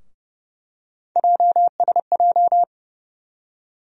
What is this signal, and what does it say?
Morse code, heard as a keyed tone, about 30 wpm, 700 Hz. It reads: JSJ
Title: USA Call Sign Postfixes - 3 Letters 30wpm